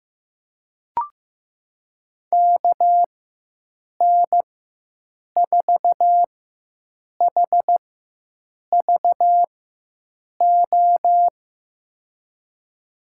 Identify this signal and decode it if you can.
Morse code, heard as a keyed tone, about 15 wpm, spaced as for 7 wpm, 700 Hz. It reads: KN4HVO
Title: USA Call Signs - 6 Characters - 4x Character Spacing - 15wpm